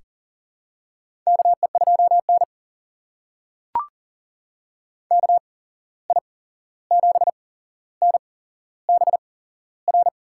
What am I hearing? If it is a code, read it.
Morse code, heard as a keyed tone, about 40 wpm, spaced as for 15 wpm, 700 Hz. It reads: KE2N KI7NBR
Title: USA Call Signs - 8x Character Spacing - 40wpm